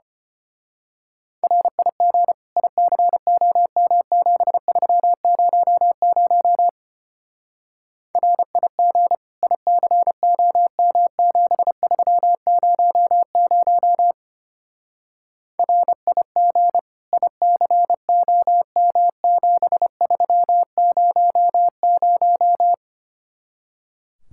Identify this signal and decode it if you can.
Morse code, heard as a keyed tone, about 29 wpm, 700 Hz. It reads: RIG ICOM7300 RIG ICOM7300 RIG ICOM7300
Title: Speed-Racing Common QSO Phrases - 25wpm